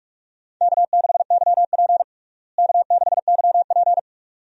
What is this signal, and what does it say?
Morse code, heard as a keyed tone, about 41 wpm, 700 Hz. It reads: K6YP K6YP
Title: USA Call Signs - 4 Characters - Repeated 2x - 45wpm